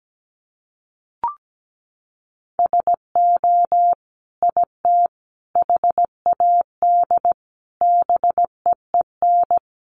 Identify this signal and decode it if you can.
Morse code, heard as a keyed tone, about 17 wpm, 700 Hz. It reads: SO IT HAD BEEN
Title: Sentences from Top 1200 Words 17wpm